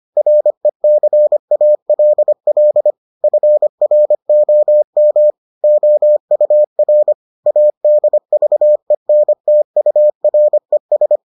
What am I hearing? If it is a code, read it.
Morse code, heard as a keyed tone, about 25 wpm, 600 Hz. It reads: RECALL FROM OUR ADVENTURES